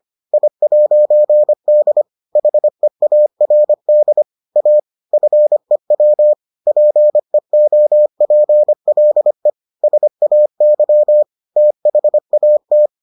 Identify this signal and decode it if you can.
Morse code, heard as a keyed tone, about 25 wpm, 600 Hz. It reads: I'D HEARD A FEW PEOPLE SAY THAT